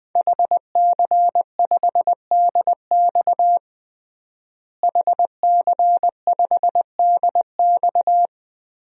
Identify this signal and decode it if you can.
Morse code, heard as a keyed tone, about 21 wpm, 700 Hz. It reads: HC5DX HC5DX
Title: DXCC-100 Call Signs - 5 Characters - Repeated 3x - 20wpm